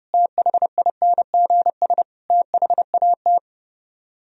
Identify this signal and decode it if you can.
Morse code, heard as a keyed tone, about 37 wpm, 700 Hz. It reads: THINGS THAT
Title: Speed-Racing - Sets of 4 Words 30wpm